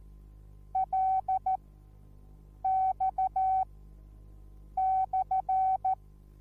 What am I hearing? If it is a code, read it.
Morse code, heard as a keyed tone, about 14 wpm, spaced as for 6 wpm, 760 Hz. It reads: LX/